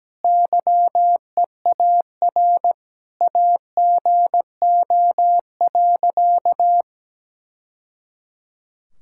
Morse code, heard as a keyed tone, about 17 wpm, 700 Hz. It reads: YEAR AGO.
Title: Sentences from Top 1000 Words - Review Part 6 - 17wpm